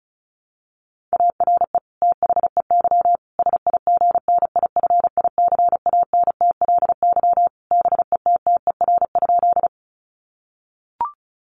Morse code, heard as a keyed tone, about 34 wpm, 700 Hz. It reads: ARE THEY SIGNIFICANTLY BETTER?
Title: Sentences from Top 1500 Words 35wpm